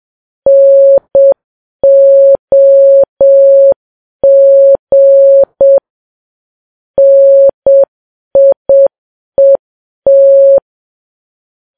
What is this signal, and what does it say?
Morse code, heard as a keyed tone, about 7 wpm, 550 Hz. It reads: NOG NIET
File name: Files Web tls / Morse on9fv / morse7kps